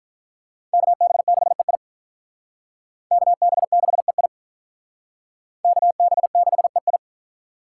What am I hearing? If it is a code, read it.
Morse code, heard as a keyed tone, about 47 wpm, 700 Hz. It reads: KB6EI KB6EI KB6EI